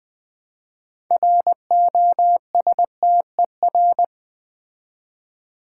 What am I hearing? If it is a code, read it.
Morse code, heard as a keyed tone, about 20 wpm, 700 Hz. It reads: ROSTER
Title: Build A Word - 3 Letters At A Time - 6-Letter Words - 20wpm